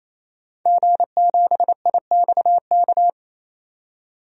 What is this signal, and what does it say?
Morse code, heard as a keyed tone, about 28 wpm, 700 Hz. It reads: G7IXK
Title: DXCC-100 Call Signs - 5 Characters - 28wpm